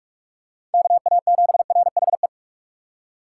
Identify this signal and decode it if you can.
Morse code, heard as a keyed tone, about 46 wpm, 700 Hz. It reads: KA7RHE